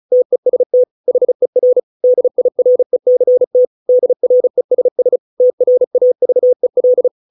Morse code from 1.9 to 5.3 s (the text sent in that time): DIRECT DRESS